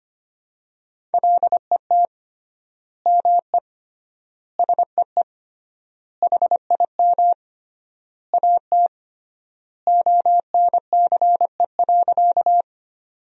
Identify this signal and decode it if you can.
Morse code, heard as a keyed tone, about 25 wpm, 700 Hz. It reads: LET ME SEE HIM AT ONCE.
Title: 3x Word-Spacing - Sentences from Top 300 Words - 25wpm